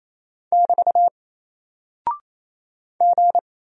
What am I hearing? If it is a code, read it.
Morse code, heard as a keyed tone, about 27 wpm, 700 Hz. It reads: <BT> G
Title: Single Character (Letter:Number:Punctuation:Prosign) 28wpm